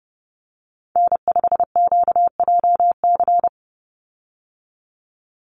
Morse code, heard as a keyed tone, about 30 wpm, 700 Hz. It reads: N5QJC
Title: USA Call Signs 30wpm